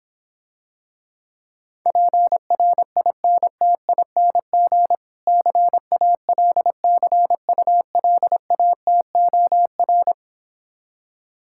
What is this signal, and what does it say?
Morse code, heard as a keyed tone, about 26 wpm, 700 Hz. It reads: PRINTING CALCULATOR